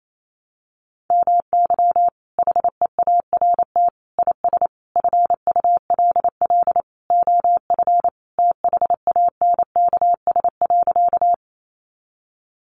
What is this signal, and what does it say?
Morse code, heard as a keyed tone, about 28 wpm, 700 Hz. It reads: MY HEART IS FULL OF THANKS.
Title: Sentences from Top 1200 Words 28wpm